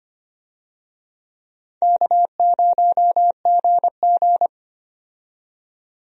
Morse code, heard as a keyed tone, about 25 wpm, 700 Hz. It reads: K0GG
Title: USA Call Signs - 4 Characters - 25wpm